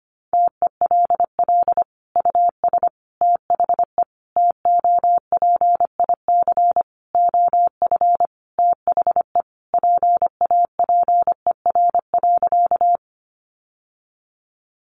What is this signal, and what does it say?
Morse code, heard as a keyed tone, about 25 wpm, 700 Hz. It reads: TELL US THE TOPIC OF THE PAPER.
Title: Sentences from Top 1200 Words 25wpm